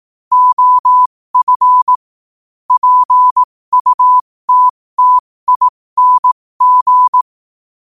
Morse code, heard as a keyed tone, about 18 wpm, spaced as for 15 wpm, 1000 Hz. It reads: OF PUTTING
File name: qotdfiles / 15wpm